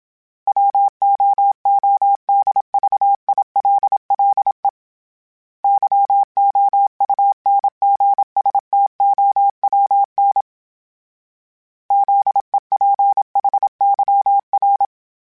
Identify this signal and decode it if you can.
Morse code, heard as a keyed tone, about 27 wpm, 800 Hz. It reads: WOODVILLE YOUNGSTOWN ZEPHYR
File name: fromSMs / FromSMs mp3s / Cities@25